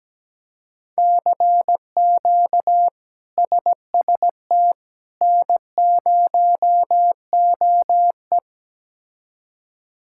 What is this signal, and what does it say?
Morse code, heard as a keyed tone, about 17 wpm, 700 Hz. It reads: CQ SST N0OE
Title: K1USN SST QSO Practice - 17wpm